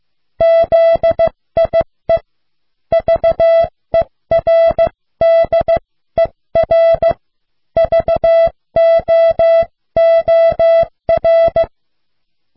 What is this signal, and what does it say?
Morse code, heard as a keyed tone, about 15 wpm, 660 Hz. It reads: ZIE VERDER VOOR